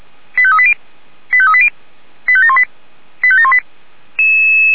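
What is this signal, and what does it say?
Five-tone selective call (ZVEI), heard as quick runs of five tones, tones of about 70 ms. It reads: 85380 85380 85718 85718
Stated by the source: ZVEI five-tone selective call received on SDR.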